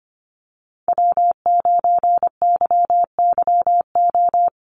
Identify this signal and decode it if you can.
Morse code, heard as a keyed tone, about 25 wpm, 700 Hz. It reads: W9YYO
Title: USA Call Signs Repeated 2x 25wpm